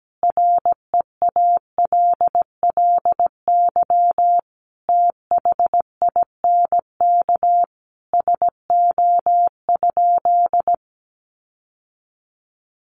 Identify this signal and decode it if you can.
Morse code, heard as a keyed tone, about 17 wpm, 700 Hz. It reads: REALLY THINK SO?
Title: Sentences from Top 600 Words 17wpm